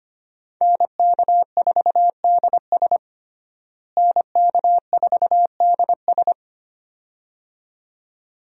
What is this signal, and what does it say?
Morse code, heard as a keyed tone, about 25 wpm, 700 Hz. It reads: NK4DS NK4DS
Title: USA Call Signs - 5 Characters - Repeated 2x - 25wpm